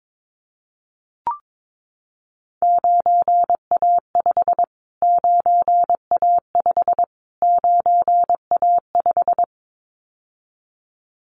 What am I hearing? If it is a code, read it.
Morse code, heard as a keyed tone, about 22 wpm, 700 Hz. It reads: 9A5 9A5 9A5
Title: Callsign Prefixes for Top 25 DXCC - Repeated 3x - 22wpm